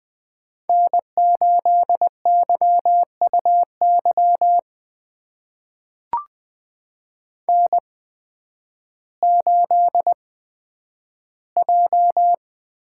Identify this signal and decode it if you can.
Morse code, heard as a keyed tone, about 20 wpm, spaced as for 11 wpm, 700 Hz. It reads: N8YUY N8J
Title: USA Call Signs - 8x Character Spacing - 20wpm